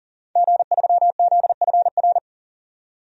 Morse code, heard as a keyed tone, about 40 wpm, 700 Hz. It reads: G3ZFR
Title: DXCC-100 Call Signs - 5 Characters - 40wpm